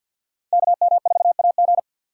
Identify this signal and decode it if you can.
Morse code, heard as a keyed tone, about 49 wpm, 700 Hz. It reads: KM4AG
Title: USA Call Signs - 5 Characters - 4x Character Spacing - 50wpm